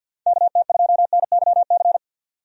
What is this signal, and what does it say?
Morse code, heard as a keyed tone, about 47 wpm, 700 Hz. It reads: KT2NYC
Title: USA Call Signs - 6 Characters - 50wpm